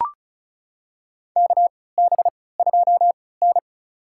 Morse code, heard as a keyed tone, about 36 wpm, spaced as for 20 wpm, 700 Hz. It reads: KB2N
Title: USA Call Signs - 3x Character Spacing - 35wpm